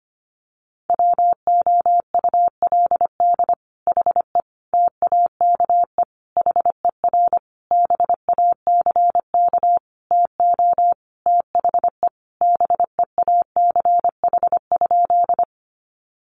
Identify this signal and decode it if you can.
Morse code, heard as a keyed tone, about 25 wpm, 700 Hz. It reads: WOULD HE TAKE HER BACK TO THE BEACH?
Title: Sentences from Top 1500 Words 25wpm